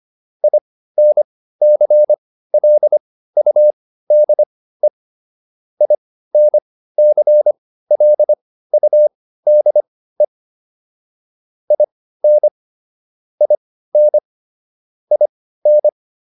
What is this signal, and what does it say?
Morse code, heard as a keyed tone, about 25 wpm, spaced as for 15 wpm, 600 Hz. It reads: INCLUDE INCLUDE IN IN IN
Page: intermediate-practice-files